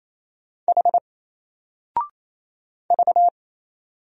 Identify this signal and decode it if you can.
Morse code, heard as a keyed tone, about 28 wpm, 700 Hz. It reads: H V
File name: Single-Letter-Number-28wpm